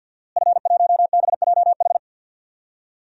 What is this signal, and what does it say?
Morse code, heard as a keyed tone, about 52 wpm, 700 Hz. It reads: F1BJH
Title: DXCC-100 Call Signs - 5 Characters - Repeated 2x - 50wpm